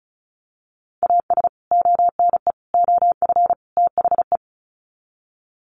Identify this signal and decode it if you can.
Morse code, heard as a keyed tone, about 34 wpm, 700 Hz. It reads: AS ONE OF THE
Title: Sets of 4 Words 35wpm